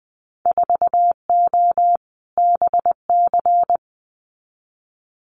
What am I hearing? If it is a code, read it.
Morse code, heard as a keyed tone, about 20 wpm, 700 Hz. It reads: 4O BC
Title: Winter Field Day Exchanges Encore 20wpm